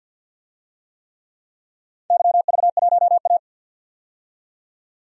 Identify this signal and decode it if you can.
Morse code, heard as a keyed tone, about 51 wpm, 700 Hz. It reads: YV1A
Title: DXCC-100 Call Signs - 4 Characters - 50wpm